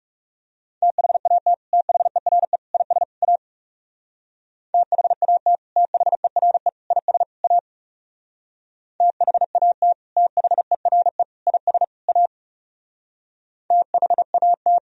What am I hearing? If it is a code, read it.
Morse code, heard as a keyed tone, about 38 wpm, 700 Hz. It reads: THAT THERE IS A THAT THERE IS A THAT THERE IS A THAT